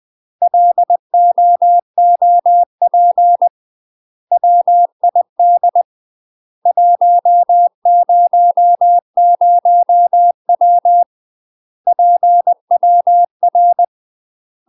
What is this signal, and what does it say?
Morse code, heard as a keyed tone, about 20 wpm, 700 Hz. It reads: LOOP WID 100W PWR
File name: UK MASTER QSOs TRAINING / 20WPM / UK MASTER QSOs 0074